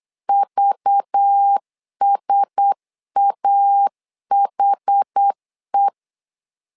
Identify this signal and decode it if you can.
Morse code, heard as a keyed tone, about 8 wpm, 800 Hz. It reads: VSAHE
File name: LEZIONI / 5 LBVC / 5 LBV 8wpm 2837